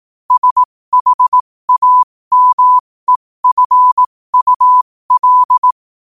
Morse code, heard as a keyed tone, about 18 wpm, spaced as for 15 wpm, 1000 Hz. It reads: SHAMEFUL